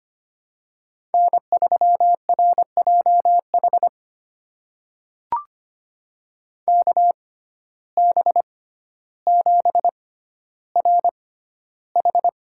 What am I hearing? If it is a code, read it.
Morse code, heard as a keyed tone, about 25 wpm, spaced as for 13 wpm, 700 Hz. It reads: N3RJH KB7RH
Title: USA Call Signs - 5 Characters - 6x Character Spacing - 25wpm